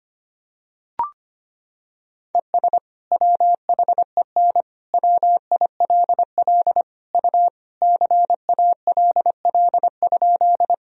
Morse code, heard as a keyed tone, about 25 wpm, 700 Hz. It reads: ES WHEN WILL U CALL?
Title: Sentences from Top 100 Words and Abbreviations 25wpm